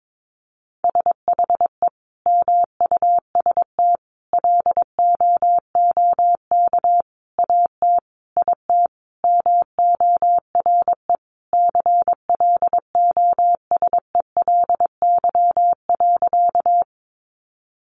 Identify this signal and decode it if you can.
Morse code, heard as a keyed tone, about 22 wpm, 700 Hz. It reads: SHE MUST LOOK AT IT MORE CLOSELY.
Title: Sentences from Top 1400 Words 22wpm